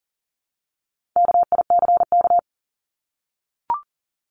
Morse code, heard as a keyed tone, about 40 wpm, 700 Hz. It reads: KICK
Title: Top 1251-1500 Words 40wpm